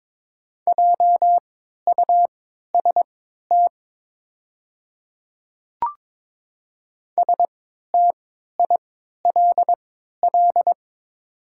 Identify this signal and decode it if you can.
Morse code, heard as a keyed tone, about 22 wpm, spaced as for 13 wpm, 700 Hz. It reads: JUST STILL
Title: 3x Character Spacing - Top 300 Words - 22wpm